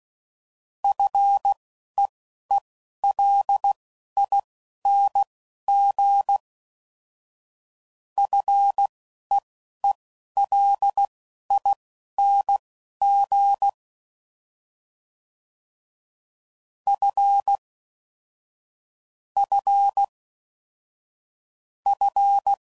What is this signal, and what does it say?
Morse code, heard as a keyed tone, about 16 wpm, spaced as for 11 wpm, 800 Hz. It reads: FEELING FEELING F F F
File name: Prefixes / FLs & THs / F Prefixes